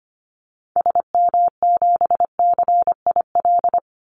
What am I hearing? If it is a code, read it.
Morse code, heard as a keyed tone, about 25 wpm, 700 Hz. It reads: SM7CIL
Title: Call Signs - 25wpm